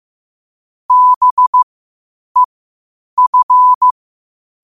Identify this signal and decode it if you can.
Morse code, heard as a keyed tone, about 15 wpm, spaced as for 8 wpm, 1000 Hz. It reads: BEF
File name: qotdfiles / 8wpm